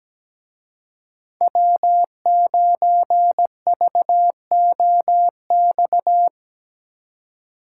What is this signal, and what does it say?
Morse code, heard as a keyed tone, about 19 wpm, 700 Hz. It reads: W9VOX
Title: Speed Racing - USA 1x3 Call Signs 17wpm